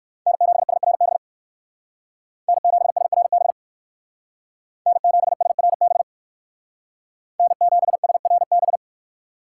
Morse code, heard as a keyed tone, about 61 wpm, 700 Hz. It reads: N7SRB N7SRB N7SRB N7SRB